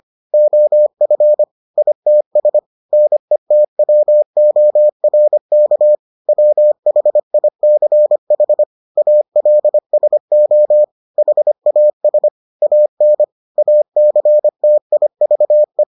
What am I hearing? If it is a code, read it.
Morse code, heard as a keyed tone, about 25 wpm, 600 Hz. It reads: OF ITS NETWORK WHICH ALSO HAS AN ACTIVE